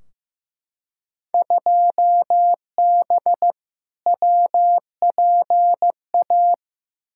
Morse code, heard as a keyed tone, about 15 wpm, 700 Hz. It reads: 2B WPA
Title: Field Day Exchanges Encore - 15wpm